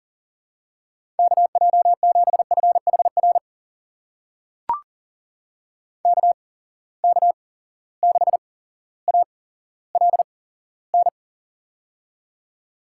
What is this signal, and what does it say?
Morse code, heard as a keyed tone, about 40 wpm, spaced as for 17 wpm, 700 Hz. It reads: KJ7FHR KK6ALN
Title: USA Call Signs - 6 Characters - 8x Character Spacing - 40wpm